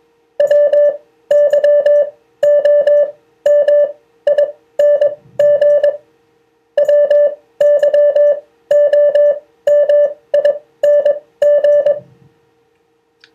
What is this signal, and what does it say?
Morse code, heard as a keyed tone, about 22 wpm, spaced as for 15 wpm, 580 Hz. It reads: WYOMING WYOMING